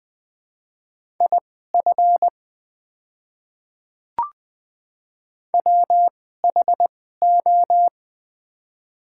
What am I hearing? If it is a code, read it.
Morse code, heard as a keyed tone, about 20 wpm, spaced as for 14 wpm, 700 Hz. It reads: IF WHO
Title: 2x Character Spacing - Top 200 Words - 20wpm